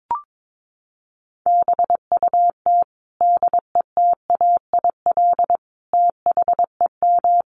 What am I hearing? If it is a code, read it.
Morse code, heard as a keyed tone, about 22 wpm, 700 Hz. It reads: BUT DETAIL THEM
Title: Sentences from Top 1500 Words 22wpm